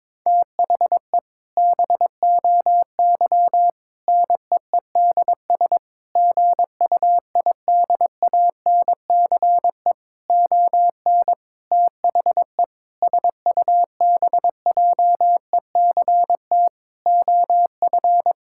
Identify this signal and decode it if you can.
Morse code, heard as a keyed tone, about 22 wpm, 700 Hz. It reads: THE BOY NEEDS GUIDANCE ON THE SUBJECT OF
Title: Sentences from Top 1600 Words 22wpm